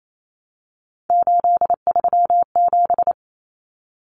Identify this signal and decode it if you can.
Morse code, heard as a keyed tone, about 28 wpm, 700 Hz. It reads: 837
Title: Numbers - 3-Digit 28wpm